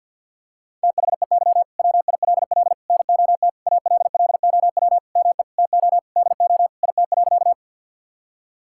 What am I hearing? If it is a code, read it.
Morse code, heard as a keyed tone, about 50 wpm, 700 Hz. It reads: THEY WILL NOT ALLOW ME TO DO IT.
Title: Sentences from Top 1000 Words - Review Part 5 - 50wpm